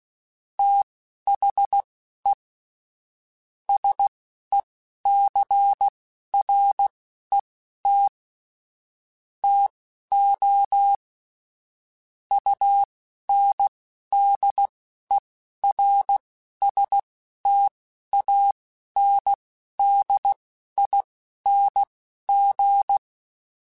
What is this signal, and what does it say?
Morse code, heard as a keyed tone, about 16 wpm, spaced as for 11 wpm, 800 Hz. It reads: THE SECRET TO UNDERSTANDING
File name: Technical Subjects / Technical Subjects-mp3s / Sw Pwr Sup @10